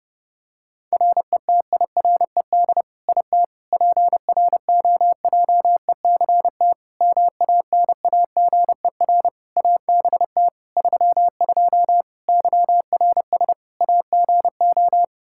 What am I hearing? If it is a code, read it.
Morse code, heard as a keyed tone, about 30 wpm, 700 Hz. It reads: RETIRED IT PROJECT MANAGER ABT 32 YRS AGO
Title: Practice QSO - Exchange 3 - More Info - 30wpm